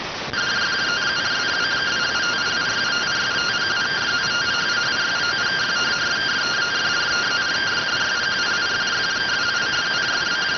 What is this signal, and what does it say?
Two-tone teleprinter signal, about 170 Hz shift.